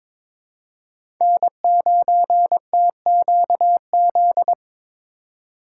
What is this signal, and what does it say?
Morse code, heard as a keyed tone, about 22 wpm, 700 Hz. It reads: N9TQZ
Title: USA Call Signs - 5 Characters - 3x Character Spacing - 22wpm